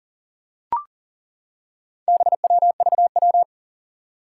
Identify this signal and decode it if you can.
Morse code, heard as a keyed tone, about 39 wpm, 700 Hz. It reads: BWVW
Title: Four Challenging Letter-Number Combinations 40wpm